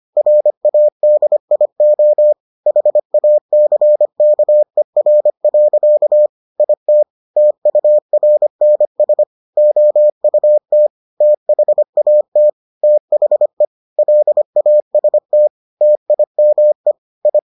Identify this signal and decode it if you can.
Morse code, heard as a keyed tone, about 25 wpm, 600 Hz. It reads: RADIO HACKER. IT TURNS OUT THAT THE LAST TIME I